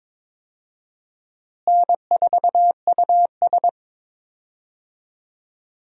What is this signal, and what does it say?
Morse code, heard as a keyed tone, about 21 wpm, 700 Hz. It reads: N4US
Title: USA Call Signs - 4 Characters - Repeated 3x - 22wpm